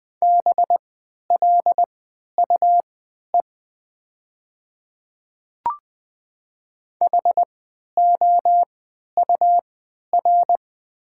Morse code, heard as a keyed tone, about 20 wpm, spaced as for 11 wpm, 700 Hz. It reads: BLUE HOUR
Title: 3x Character Spacing - Top 500 Words - 20wpm